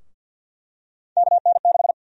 Morse code, heard as a keyed tone, about 52 wpm, 700 Hz. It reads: KN6
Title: USA Call Sign Prefixes - 50wpm